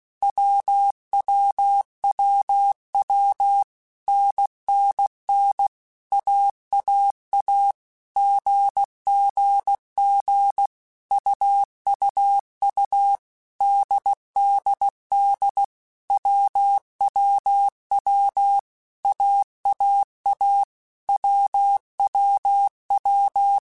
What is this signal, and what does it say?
Morse code, heard as a keyed tone, about 16 wpm, 800 Hz. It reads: WWWW NNN AAA GGG UUU DDD WWW AAA WWW
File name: Basic Letter Groups By7Groups / AWUNDG / AWUNDG@15